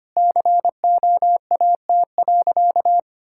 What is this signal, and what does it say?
Morse code, heard as a keyed tone, about 25 wpm, 700 Hz. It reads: COAT.
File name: practice / 25wpm / Sentences with 4-Word Phrases - 25wpm